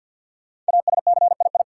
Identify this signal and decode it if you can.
Morse code, heard as a keyed tone, about 52 wpm, 700 Hz. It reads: ASCII
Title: Sets of 5 Words - New Word Drill - 50wpm